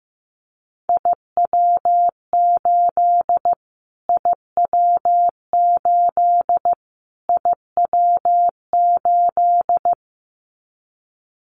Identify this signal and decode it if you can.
Morse code, heard as a keyed tone, about 15 wpm, 700 Hz. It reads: IW8 IW8 IW8
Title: Callsign Prefixes for Top 25 DXCC - Repeated 3x - 15wpm